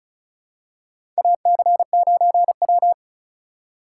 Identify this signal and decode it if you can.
Morse code, heard as a keyed tone, about 34 wpm, 700 Hz. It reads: AC9W
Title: USA Call Signs - 4 Characters - 6x Character Spacing - 35wpm